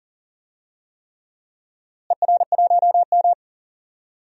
Morse code, heard as a keyed tone, about 40 wpm, 700 Hz. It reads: ER1M
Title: DXCC-100 Call Signs - 4 Characters - Repeated 3x - 40wpm